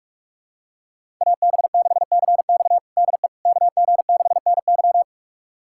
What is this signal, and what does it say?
Morse code, heard as a keyed tone, about 46 wpm, 700 Hz. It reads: AB6CX DE KG6NY
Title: USA Callsign DE Callsign - 5 Characters - 45wpm